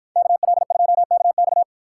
Morse code, heard as a keyed tone, about 53 wpm, 700 Hz. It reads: KL2KX